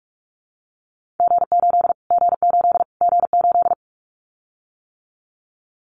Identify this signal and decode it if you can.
Morse code, heard as a keyed tone, about 44 wpm, 700 Hz. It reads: G8 G8 G8
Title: Callsign Prefixes for Top 25 DXCC - Repeated 3x - 45wpm